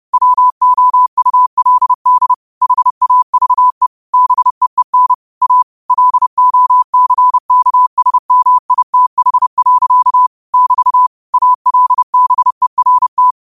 Morse code, heard as a keyed tone, about 30 wpm, 1000 Hz. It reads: WOULD HAVE BEEN A LOCKSMITH. <BT> ALBERT
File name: qotdfiles / 30wpm